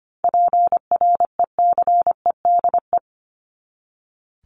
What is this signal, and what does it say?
Morse code, heard as a keyed tone, about 25 wpm, 700 Hz. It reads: PRECEDE
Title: ING Suffix - Top 100 Words with Base Words 25wpm